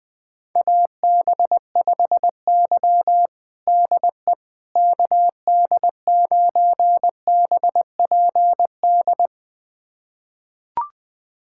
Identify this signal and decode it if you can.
Morse code, heard as a keyed tone, about 20 wpm, 700 Hz. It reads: AB5Y DE KD9BPD
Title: USA Callsign DE Callsign 20wpm